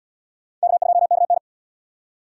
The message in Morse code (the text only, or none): K2KN